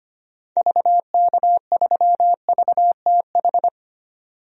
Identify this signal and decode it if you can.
Morse code, heard as a keyed tone, about 25 wpm, 700 Hz. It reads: VK3VTH
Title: DXCC-100 Call Signs - 6 Characters - 25wpm